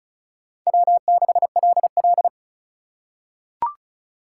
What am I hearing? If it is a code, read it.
Morse code, heard as a keyed tone, about 36 wpm, 700 Hz. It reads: W6LL
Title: USA Call Signs - 4 Characters - 2x Character Spacing - 35wpm